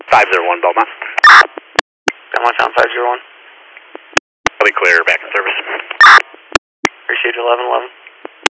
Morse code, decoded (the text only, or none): none